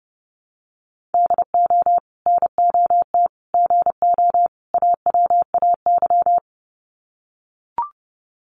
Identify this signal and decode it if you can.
Morse code, heard as a keyed tone, about 29 wpm, 700 Hz. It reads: DO NOT GO AWAY
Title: Sets of 4 Words 30wpm